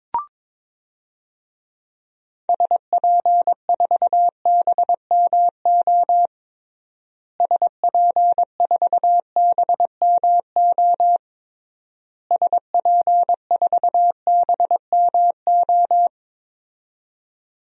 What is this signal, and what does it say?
Morse code, heard as a keyed tone, about 22 wpm, 700 Hz. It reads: SP4BMO SP4BMO SP4BMO
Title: DXCC-100 Call Signs - 6 Characters - Repeated 3x - 22wpm